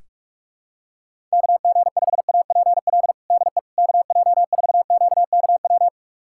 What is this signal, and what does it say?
Morse code, heard as a keyed tone, about 41 wpm, 700 Hz. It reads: KG5APL DE KJ4QKW
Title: USA Callsign DE Callsign 45wpm